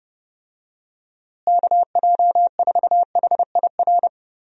Morse code, heard as a keyed tone, about 30 wpm, 700 Hz. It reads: KJ4HIR
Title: USA Call Signs - 6 Characters - Repeated 3x - 30wpm